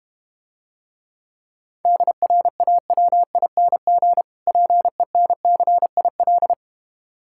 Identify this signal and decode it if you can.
Morse code, heard as a keyed tone, about 32 wpm, 700 Hz. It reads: DRAWING PENCIL